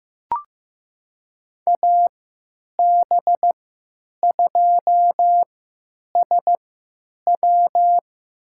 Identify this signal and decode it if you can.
Morse code, heard as a keyed tone, about 15 wpm, spaced as for 9 wpm, 700 Hz. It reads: AB2SW
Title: USA Call Signs - 5 Characters - 3x Character Spacing - 15wpm